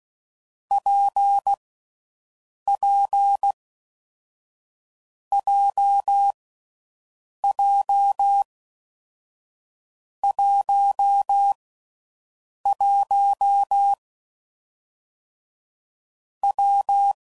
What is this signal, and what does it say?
Morse code, heard as a keyed tone, about 16 wpm, 800 Hz. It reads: P P J J 1 1 W
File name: Combinations / Difficult Combos / WPJ WPJ1